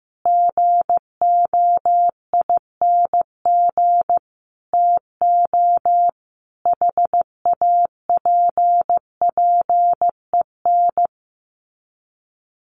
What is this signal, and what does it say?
Morse code, heard as a keyed tone, about 15 wpm, 700 Hz. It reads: GOING TO HAPPEN
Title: Sets of 5 Words Encore 15wpm